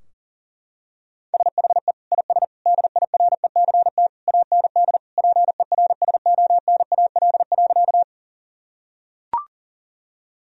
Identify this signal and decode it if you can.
Morse code, heard as a keyed tone, about 40 wpm, 700 Hz. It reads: SHE IS DIRECT AND PERSONAL.
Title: Sentences from Top 1000 Words - Review Part 4 - 40wpm